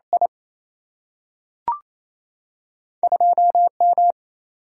Morse code, heard as a keyed tone, about 28 wpm, 700 Hz. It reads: I 2 M
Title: Single-Letter Single-Number 28wpm